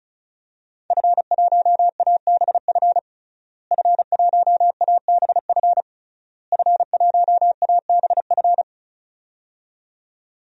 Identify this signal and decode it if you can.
Morse code, heard as a keyed tone, about 36 wpm, 700 Hz. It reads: F1ABF F1ABF F1ABF
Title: DXCC-100 Call Signs - 5 Characters - Repeated 3x - 35wpm